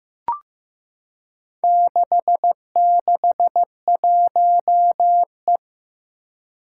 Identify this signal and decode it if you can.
Morse code, heard as a keyed tone, about 15 wpm, 700 Hz. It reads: 661E
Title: Four Challenging Letter-Number Combinations 15wpm